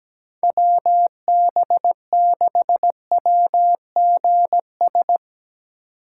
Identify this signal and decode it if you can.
Morse code, heard as a keyed tone, about 19 wpm, 700 Hz. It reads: WB6WGS
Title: Speed Racing - USA 2x3 Call Signs 17wpm